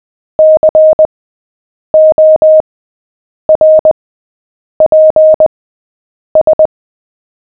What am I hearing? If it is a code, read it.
Morse code, heard as a keyed tone, about 20 wpm, spaced as for 8 wpm, 620 Hz. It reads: CORPS
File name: cw / fists / Short-Story-7-8wpm